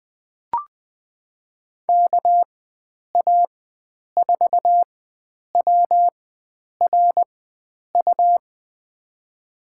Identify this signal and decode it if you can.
Morse code, heard as a keyed tone, about 20 wpm, spaced as for 9 wpm, 700 Hz. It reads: KA4WRU
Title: USA Call Signs - 6 Characters - 4x Character Spacing - 20wpm